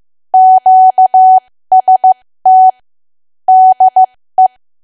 Morse code, heard as a keyed tone, about 15 wpm, spaced as for 13 wpm, 750 Hz. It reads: QST DE
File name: cwa / W1AW-Level2 / 13WPM